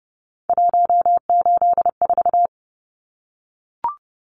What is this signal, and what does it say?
Morse code, heard as a keyed tone, about 30 wpm, 700 Hz. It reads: 184
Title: Numbers - 3-Digit 30wpm